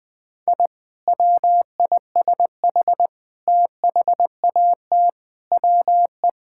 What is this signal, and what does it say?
Morse code, heard as a keyed tone, about 20 wpm, 700 Hz. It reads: I WISH THAT WE
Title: Sentences with 2-Word Phrases - 20wpm